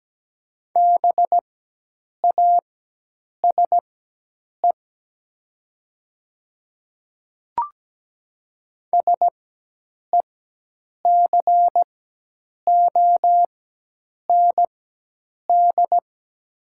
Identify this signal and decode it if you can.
Morse code, heard as a keyed tone, about 17 wpm, spaced as for 8 wpm, 700 Hz. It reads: BASE SECOND
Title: 4x Character Spacing - Top 300 Words - 17wpm